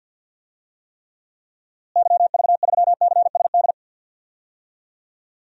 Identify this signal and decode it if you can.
Morse code, heard as a keyed tone, about 46 wpm, 700 Hz. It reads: YV3CSD
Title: DXCC-100 Call Signs - 6 Characters - Repeated 2x - 50wpm